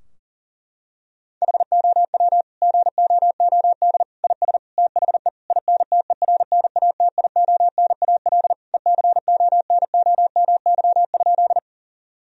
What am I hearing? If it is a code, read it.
Morse code, heard as a keyed tone, about 37 wpm, 700 Hz. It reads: HOW GOOD IS THE INTERNATIONAL ECONOMY?
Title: Sentences from Top 1000 Words - Review Part 4 - 40wpm